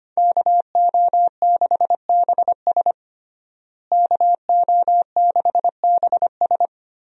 Morse code, heard as a keyed tone, about 25 wpm, 700 Hz. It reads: KO6BS KO6BS
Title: USA Call Signs - 5 Characters - Repeated 3x - 25wpm